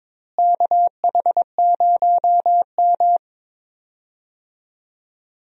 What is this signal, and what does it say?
Morse code, heard as a keyed tone, about 22 wpm, 700 Hz. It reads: KH0M
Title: USA Call Signs - 4 Characters - Repeated 2x - 22wpm